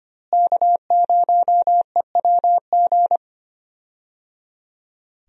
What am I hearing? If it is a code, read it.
Morse code, heard as a keyed tone, about 25 wpm, 700 Hz. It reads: K0EWG
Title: USA Call Signs - 5 Characters - Repeated 3x - 25wpm